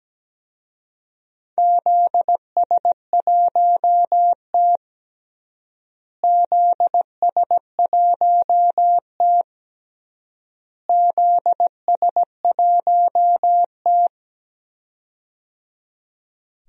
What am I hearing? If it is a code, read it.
Morse code, heard as a keyed tone, about 17 wpm, 700 Hz. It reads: ZS1T ZS1T ZS1T
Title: DXCC-100 Call Signs - 4 Characters - Repeated 3x - 17wpm